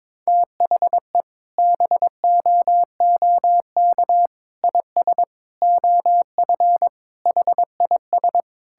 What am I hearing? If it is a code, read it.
Morse code, heard as a keyed tone, about 22 wpm, 700 Hz. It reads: THE BOOK IS OF HIS
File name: practice / 22wpm / Sentences with 2-Word Phrases - 22wpm